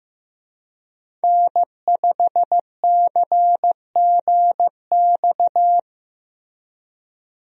Morse code, heard as a keyed tone, about 15 wpm, 700 Hz. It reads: N5CGX
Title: USA Call Signs - 5 Characters - Repeated 3x - 15wpm